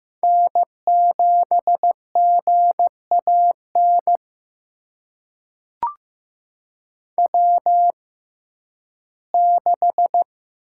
Morse code, heard as a keyed tone, about 15 wpm, spaced as for 11 wpm, 700 Hz. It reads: N7GAN W6
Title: USA Call Signs - 5 Characters - 6x Character Spacing - 15wpm